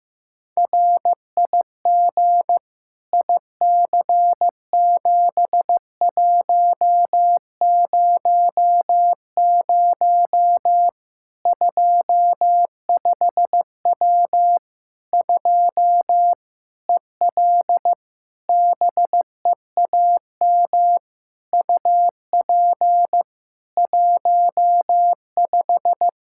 Morse code, heard as a keyed tone, about 15 wpm, 700 Hz. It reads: RIG IC7100 25W 2 EL BEAM UP 15
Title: K1USN SST QSO Practice 15wpm